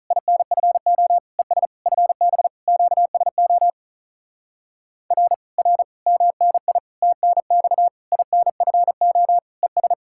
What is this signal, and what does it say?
Morse code, heard as a keyed tone, about 48 wpm, 700 Hz. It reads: INFO ES FB QSO R R MNI TNX INFO ES